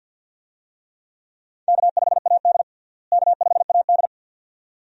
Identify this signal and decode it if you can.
Morse code, heard as a keyed tone, about 52 wpm, 700 Hz. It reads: K5AD K5AD
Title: USA Call Signs - 4 Characters - Repeated 2x - 50wpm